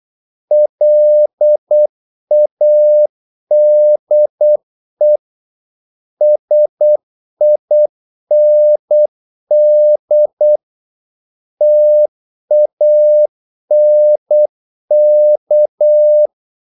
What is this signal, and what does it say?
Morse code, heard as a keyed tone, about 8 wpm, 600 Hz. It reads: LADE SIND TANK
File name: Korte ord DK 8wpm